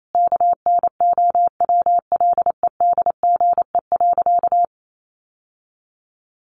Morse code, heard as a keyed tone, about 28 wpm, 700 Hz. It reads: KNOWLEDGE.
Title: Sentences from Top 1100 Words 28wpm